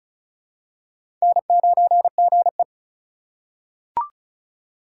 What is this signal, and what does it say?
Morse code, heard as a keyed tone, about 36 wpm, 700 Hz. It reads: N9GE
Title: USA Call Signs - 4x Character Spacing - 35wpm